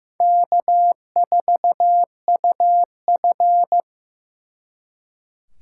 Morse code, heard as a keyed tone, about 18 wpm, 700 Hz. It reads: K4UF
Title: Speed Racing - USA 1x2 Call Signs 15wpm